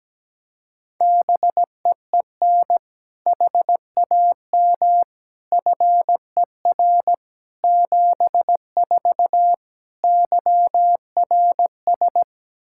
Morse code, heard as a keyed tone, about 17 wpm, 700 Hz. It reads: BEEN HAM FER 74 YRS
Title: Practice QSO - Exchange 3 - More Info - 17wpm